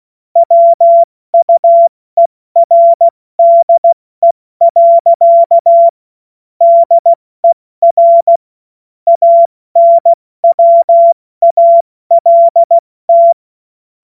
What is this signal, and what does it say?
Morse code, heard as a keyed tone, about 16 wpm, spaced as for 14 wpm, 680 Hz. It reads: WUERDE. DER ANWALT
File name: Morsekurs / Witze / Witz 30. August 2022 70 F80